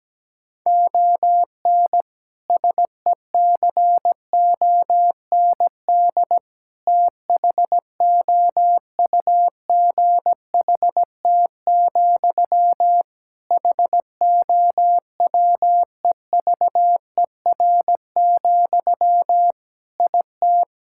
Morse code, heard as a keyed tone, about 17 wpm, 700 Hz. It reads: ON SECOND THOUGHT, HOWEVER, IT
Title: Sentences from Top 1600 Words 17wpm